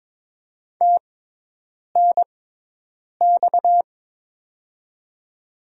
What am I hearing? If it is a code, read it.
Morse code, heard as a keyed tone, about 22 wpm, spaced as for 8 wpm, 700 Hz. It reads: TNX
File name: Top 90 QSO Elements - 6x Character Spacing - 22wpm